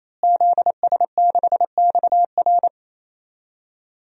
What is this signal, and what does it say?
Morse code, heard as a keyed tone, about 28 wpm, 700 Hz. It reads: ZS6XR
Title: DXCC-100 Call Signs - 5 Characters - Repeated 3x - 28wpm